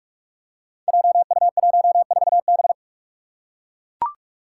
Morse code, heard as a keyed tone, about 44 wpm, 700 Hz. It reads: JU14B
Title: Five Challenging Letter-Number Combinations 45wpm